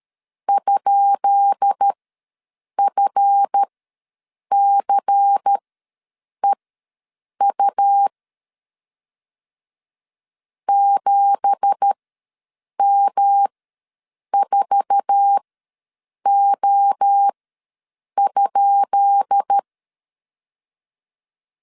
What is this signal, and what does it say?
Morse code, heard as a keyed tone, about 13 wpm, spaced as for 7 wpm, 800 Hz. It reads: ?FCEU 7M4O?
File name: LEZIONI / 10 SIGNS / 10B INT ws 12wpm 3527